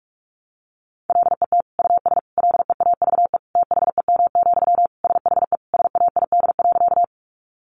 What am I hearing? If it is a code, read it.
Morse code, heard as a keyed tone, about 46 wpm, 700 Hz. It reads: LET US LEAVE THEM, SHE SAID.
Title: Sentences from Top 400 Words 45wpm